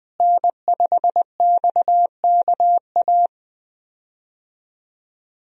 Morse code, heard as a keyed tone, about 20 wpm, 700 Hz. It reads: N5XKA
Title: USA Call Signs - 5 Characters - Repeated 2x - 20wpm